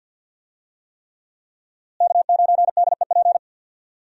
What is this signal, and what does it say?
Morse code, heard as a keyed tone, about 52 wpm, 700 Hz. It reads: K9DEP
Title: USA Call Signs - 5 Characters - Repeated 3x - 50wpm